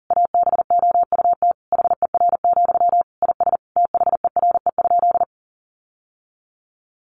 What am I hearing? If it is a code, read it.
Morse code, heard as a keyed tone, about 40 wpm, 700 Hz. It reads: ABOUT HER, IS THERE?
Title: Sentences from Top 1500 Words 40wpm